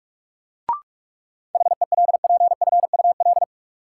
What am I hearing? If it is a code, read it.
Morse code, heard as a keyed tone, about 43 wpm, 700 Hz. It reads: HELPFUL
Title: Sentences from Top 1000 Words - Review Part 6 - New Word Drill - 45wpm